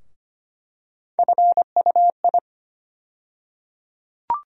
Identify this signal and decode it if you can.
Morse code, heard as a keyed tone, about 25 wpm, 700 Hz. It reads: FUI
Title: USA Call Sign Postfixes - 3 Letters 25wpm